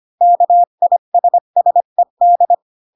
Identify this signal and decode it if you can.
Morse code, heard as a keyed tone, about 25 wpm, 700 Hz. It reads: KISSED
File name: Pinocchio / Pinocchio ch5 20 25 ik1hge